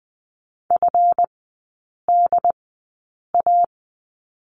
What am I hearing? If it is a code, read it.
Morse code, heard as a keyed tone, about 20 wpm, spaced as for 8 wpm, 700 Hz. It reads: FDA
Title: 184 Lesson 30e - Letter b - Three Characters 20wpm